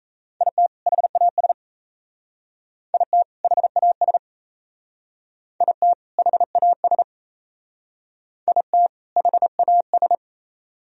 Morse code, heard as a keyed tone, about 34 wpm, 700 Hz. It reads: IT HAS IT HAS IT HAS IT HAS
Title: Speed-Racing - Sets of 2 Words 28wpm